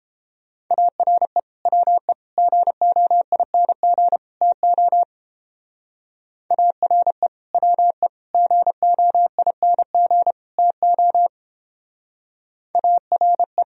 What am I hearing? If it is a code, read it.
Morse code, heard as a keyed tone, about 31 wpm, 700 Hz. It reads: ARE WE GOING TO ARE WE GOING TO ARE